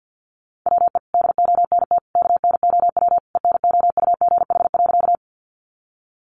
Morse code, heard as a keyed tone, about 50 wpm, 700 Hz. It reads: WE DONT KNOW ENOUGH.
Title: Sentences from Top 400 Words 50wpm